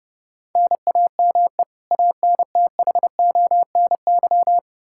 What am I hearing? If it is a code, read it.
Morse code, heard as a keyed tone, about 30 wpm, 700 Hz. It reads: NAME ANTHONY
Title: K1USN SST QSO Practice 30wpm